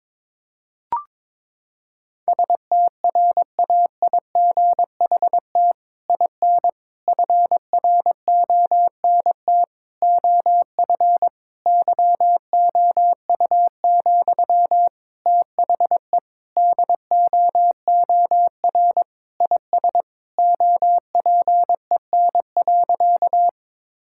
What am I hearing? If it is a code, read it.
Morse code, heard as a keyed tone, about 22 wpm, 700 Hz. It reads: STRAIGHT IN FRONT OF YOU, THE DOOR IS OPEN.
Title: Sentences from Top 1000 Words - Review Part 6 - 22wpm